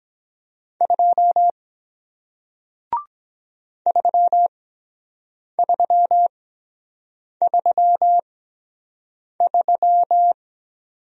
Morse code, heard as a keyed tone, about 23 wpm, 700 Hz. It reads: 2 3 3 3 3